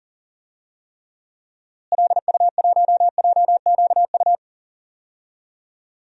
Morse code, heard as a keyed tone, about 40 wpm, 700 Hz. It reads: LU1JQU
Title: DXCC-100 Call Signs - 6 Characters - Repeated 3x - 40wpm